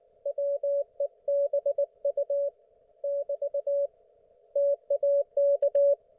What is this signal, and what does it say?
Morse code, heard as a keyed tone, about 18 wpm, 560 Hz. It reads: WEBU <BT> TAK